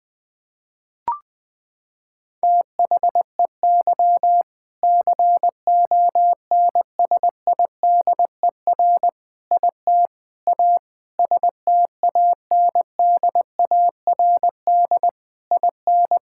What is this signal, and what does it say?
Morse code, heard as a keyed tone, about 19 wpm, 700 Hz. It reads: THEY CONSIDER IT A STANDARD IN
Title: Sentences from Top 1000 Words - Review Part 6 - 20wpm